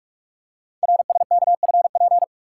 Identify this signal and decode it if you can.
Morse code, heard as a keyed tone, about 44 wpm, 700 Hz. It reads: RSKFP
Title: Code Groups - 5 Letters - 45wpm